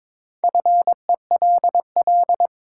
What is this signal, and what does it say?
Morse code, heard as a keyed tone, about 22 wpm, 700 Hz. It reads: FELL
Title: Sentences from Top 1600 Words - New Word Drill - 22wpm